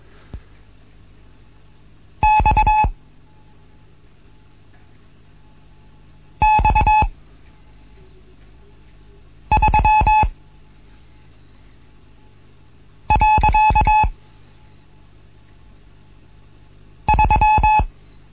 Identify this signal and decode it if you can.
Morse code, heard as a keyed tone, about 21 wpm, 860 Hz. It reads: X X 3 . 3
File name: Morsen / Kurs A 15